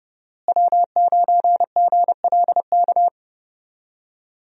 Morse code, heard as a keyed tone, about 32 wpm, 700 Hz. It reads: W9GLK
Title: USA Call Signs - 5 Characters - Repeated 2x - 30wpm